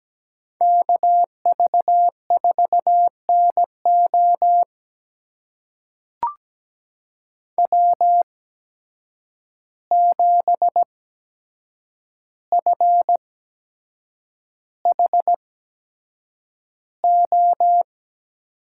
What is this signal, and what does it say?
Morse code, heard as a keyed tone, about 17 wpm, spaced as for 7 wpm, 700 Hz. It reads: KV4NO W7FHO
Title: USA Call Signs - 5 Characters - 8x Character Spacing - 17wpm